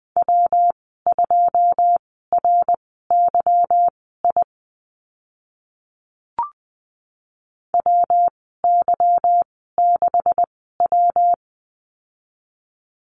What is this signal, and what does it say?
Morse code, heard as a keyed tone, about 20 wpm, spaced as for 14 wpm, 700 Hz. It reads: W2RYI WY6W
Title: 154 Lesson 25g - Letter p - Call Signs 20wpm